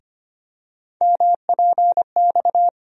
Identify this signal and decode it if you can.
Morse code, heard as a keyed tone, about 25 wpm, 700 Hz. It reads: MPX
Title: USA Call Sign Postfixes - 3 Letters 25wpm